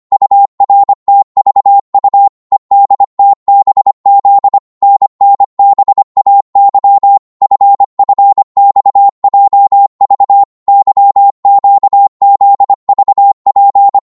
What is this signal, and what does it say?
Morse code, heard as a keyed tone, about 25 wpm, 800 Hz. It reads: URTVU EDTBZ NNBAY FFXJV YQZVP